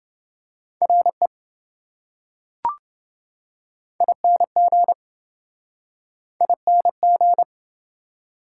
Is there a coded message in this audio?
Yes: Morse code from 0.8 to 7.4 s, heard as a keyed tone, about 26 wpm, 700 Hz. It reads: RE ING ING